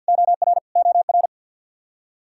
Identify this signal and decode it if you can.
Morse code, heard as a keyed tone, about 51 wpm, 700 Hz. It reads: OR OR
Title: Practice QSO - Exchange 1 - Beginning - 50wpm